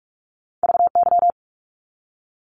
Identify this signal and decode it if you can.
Morse code, heard as a keyed tone, about 44 wpm, 700 Hz. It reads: VY
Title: Abbreviations Repeated 3x - 45wpm